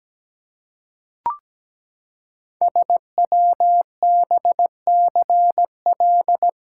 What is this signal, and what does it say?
Morse code, heard as a keyed tone, about 17 wpm, 700 Hz. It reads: SWBCL
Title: Code Groups - 5 Letters - 17wpm